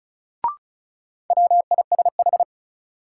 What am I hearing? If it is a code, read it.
Morse code, heard as a keyed tone, about 36 wpm, 700 Hz. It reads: WISH
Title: Sentences from Top 1600 Words - New Word Drill Alt - 35wpm